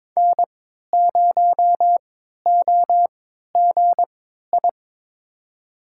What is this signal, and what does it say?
Morse code, heard as a keyed tone, about 22 wpm, spaced as for 13 wpm, 700 Hz. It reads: N0OGI
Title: USA Call Signs - 5 Characters - 3x Character Spacing - 22wpm